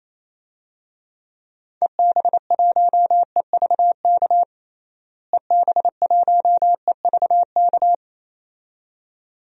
Morse code, heard as a keyed tone, about 28 wpm, 700 Hz. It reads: EB1EVK EB1EVK
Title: DXCC-100 Call Signs - 6 Characters - Repeated 2x - 28wpm